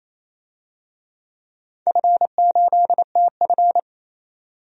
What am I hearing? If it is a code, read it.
Morse code, heard as a keyed tone, about 28 wpm, 700 Hz. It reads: F8TF
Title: DXCC-100 Call Signs - 4 Characters - 28wpm